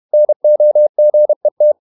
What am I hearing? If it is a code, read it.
Morse code, heard as a keyed tone, about 31 wpm, 600 Hz. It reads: NOGET
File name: Ferie QRQ 31wpm